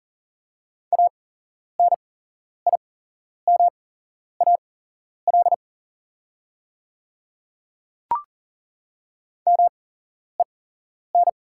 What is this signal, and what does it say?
Morse code, heard as a keyed tone, about 40 wpm, spaced as for 11 wpm, 700 Hz. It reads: ANIMAL MEN
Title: 8x Character Spacing - Top 200 Words - 40wpm